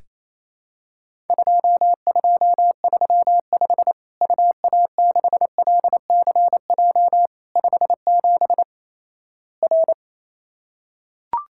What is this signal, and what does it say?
Morse code, heard as a keyed tone, about 28 wpm, 700 Hz. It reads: 2235 UA6LCJ 57 R
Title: Worked All Europe DX Contest - QTC Exchange - 28wpm